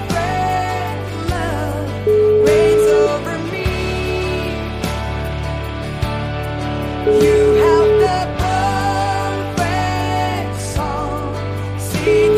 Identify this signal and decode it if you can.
Ringback tone (Europe, 425 Hz), with noise.